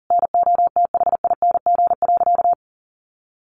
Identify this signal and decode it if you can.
Morse code, heard as a keyed tone, about 40 wpm, 700 Hz. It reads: NOTHING.
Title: Sentences from Top 1200 Words 40wpm